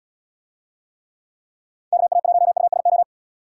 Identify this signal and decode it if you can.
Morse code, heard as a keyed tone, about 77 wpm, 700 Hz. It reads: KA1USO